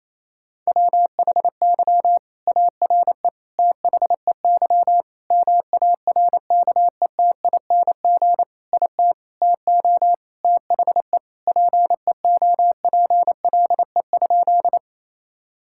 Morse code, heard as a keyed tone, about 28 wpm, 700 Hz. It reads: WHY ARE THEY MARKETING IT TO THE PEOPLE?
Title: Sentences from Top 1000 Words - Review Part 5 - 28wpm